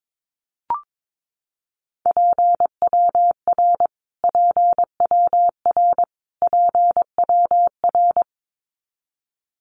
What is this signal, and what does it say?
Morse code, heard as a keyed tone, about 22 wpm, 700 Hz. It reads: PWR PWR PWR
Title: Abbreviations Repeated 3x - 22wpm